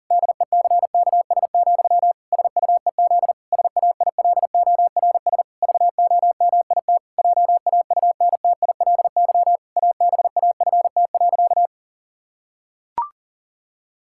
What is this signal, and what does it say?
Morse code, heard as a keyed tone, about 37 wpm, 700 Hz. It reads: DECKS, SUEZ SAILORS VOMIT JAUNTILY ABAFT.
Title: Pangram Challenge v2 - 40wpm